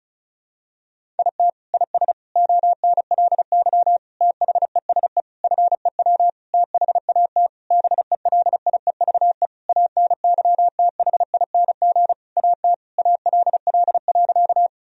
Morse code, heard as a keyed tone, about 36 wpm, 700 Hz. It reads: IT IS ONLY THESE FEW THAT BELIEVE ANYTHING AT ALL.
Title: Sentences from Top 1000 Words - Review Part 1 - 35wpm